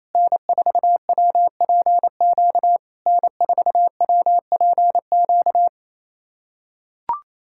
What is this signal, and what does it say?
Morse code, heard as a keyed tone, about 28 wpm, 690 Hz. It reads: N4WPQ N4WPQ
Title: SOTA QSO Practice 28wpm